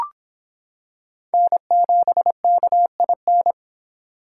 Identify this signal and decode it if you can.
Morse code, heard as a keyed tone, about 26 wpm, 700 Hz. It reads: N7KIN